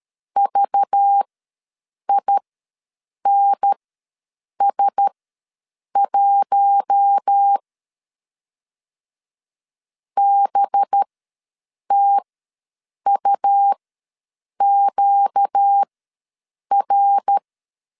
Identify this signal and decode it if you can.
Morse code, heard as a keyed tone, about 13 wpm, spaced as for 7 wpm, 800 Hz. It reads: VINS1 BTUQR
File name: LEZIONI / 5 LBVC / 5 LBVC ws 12wpm 3159